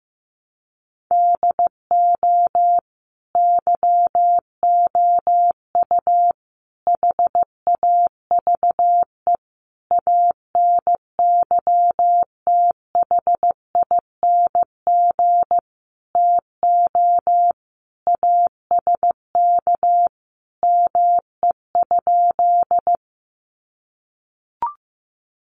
Morse code, heard as a keyed tone, about 14 wpm, 700 Hz. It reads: DO YOU HAVE ANYTHING TO ASK ME?
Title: Sentences from Top 1000 Words 15wpm